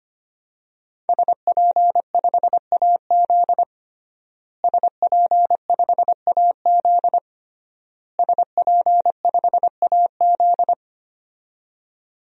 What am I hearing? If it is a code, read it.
Morse code, heard as a keyed tone, about 25 wpm, 700 Hz. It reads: SP5AZ SP5AZ SP5AZ
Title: DXCC-100 Call Signs - 5 Characters - Repeated 3x - 25wpm